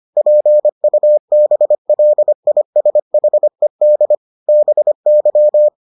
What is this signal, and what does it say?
Morse code, heard as a keyed tone, about 25 wpm, 600 Hz. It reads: PUBLISHED BY